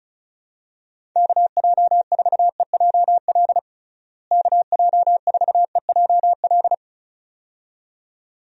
Morse code, heard as a keyed tone, about 34 wpm, 700 Hz. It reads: KJ4EJL KJ4EJL
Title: USA Call Signs - 6 Characters - Repeated 2x - 35wpm